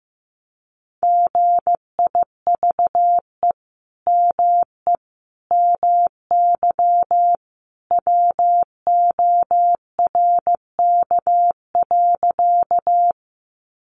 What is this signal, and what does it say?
Morse code, heard as a keyed tone, about 15 wpm, 700 Hz. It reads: GIVE ME MY WORK.
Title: Sentences from Top 200 Words 15wpm